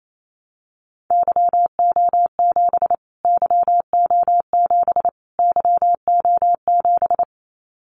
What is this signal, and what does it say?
Morse code, heard as a keyed tone, about 28 wpm, 700 Hz. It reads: YO7 YO7 YO7
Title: Callsign Prefixes for Top 25 DXCC - Repeated 3x - 28wpm